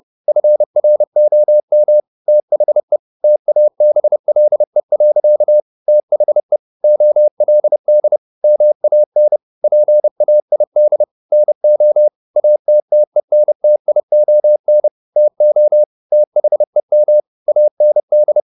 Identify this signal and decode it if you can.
Morse code, heard as a keyed tone, about 30 wpm, 600 Hz. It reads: FROM THE TABLE. THE OLD MAN PAID NO ATTENTION TO THEM AND